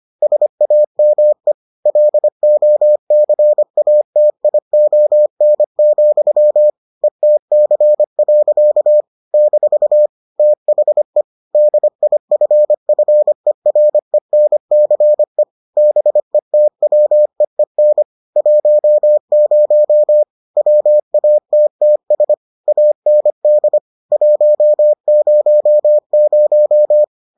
Morse code, heard as a keyed tone, about 25 wpm, 600 Hz. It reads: SAME LOCATION, ETC. - THE DIFFERENCE BETWEEN 10 WATTS AND 100